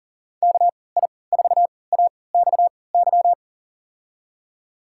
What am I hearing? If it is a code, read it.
Morse code, heard as a keyed tone, about 40 wpm, spaced as for 23 wpm, 700 Hz. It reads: KI4AXY
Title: USA Call Signs - 3x Character Spacing - 40wpm